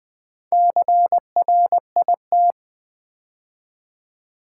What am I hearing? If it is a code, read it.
Morse code, heard as a keyed tone, about 20 wpm, 700 Hz. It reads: CRIT
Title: Build A Word - 2 Letters At A Time - 8-Letter Words - 20wpm